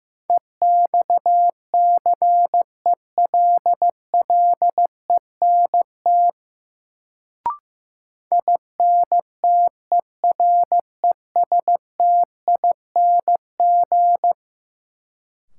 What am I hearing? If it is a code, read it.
Morse code, heard as a keyed tone, about 16 wpm, 700 Hz. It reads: EXCELLENT INTERESTING
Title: Sentences from Top 1000 Words - New Word Drill - 15wpm